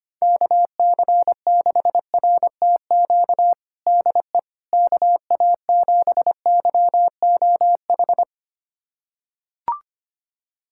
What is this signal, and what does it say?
Morse code, heard as a keyed tone, about 25 wpm, 700 Hz. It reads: KC6RTQ DE KA7YOH
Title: USA Callsign DE Callsign - 6 Characters - 25wpm